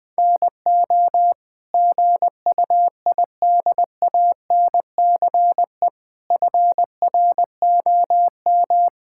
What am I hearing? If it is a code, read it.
Morse code, heard as a keyed tone, about 21 wpm, 700 Hz. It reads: NO GUIDANCE FROM
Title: Sentences from Top 1600 Words 20wpm